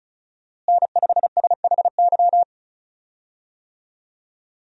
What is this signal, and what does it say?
Morse code, heard as a keyed tone, about 36 wpm, 700 Hz. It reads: N5SHY
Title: USA Call Signs - 5 Characters - Repeated 2x - 35wpm